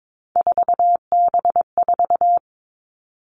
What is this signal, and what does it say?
Morse code, heard as a keyed tone, about 22 wpm, 700 Hz. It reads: 4B4
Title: Three Challenging Letter-Number Combinations 22wpm